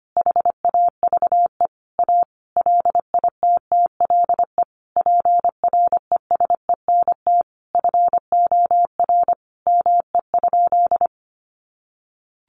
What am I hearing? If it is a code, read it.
Morse code, heard as a keyed tone, about 25 wpm, 700 Hz. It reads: HAVE A LITTLE PRESENT FOR ME?
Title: Sentences from Top 800 Words 25wpm